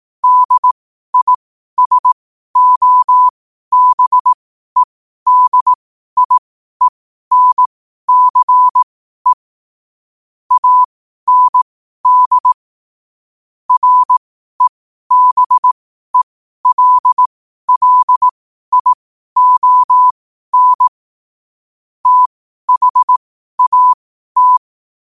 Morse code, heard as a keyed tone, about 18 wpm, spaced as for 12 wpm, 1000 Hz. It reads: DISOBEDIENCE AND REBELLION THAT
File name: qotdfiles / 12wpm